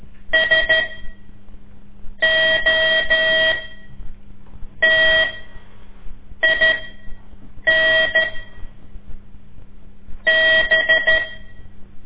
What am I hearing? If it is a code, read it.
Morse code, heard as a keyed tone, about 13 wpm, spaced as for 6 wpm, 1900 Hz. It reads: SOTIN B
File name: ik1weg2 / CORSI CW / lez5-2